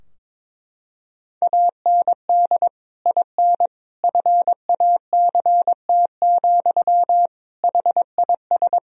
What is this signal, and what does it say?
Morse code, heard as a keyed tone, about 22 wpm, 700 Hz. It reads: AND IN FACT, HIS
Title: Sentences from Top 1000 Words - Review Part 3 - 22wpm